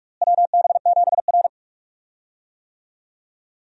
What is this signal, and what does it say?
Morse code, heard as a keyed tone, about 43 wpm, 700 Hz. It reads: WB7R
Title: USA Call Signs - 4 Characters - Repeated 2x - 45wpm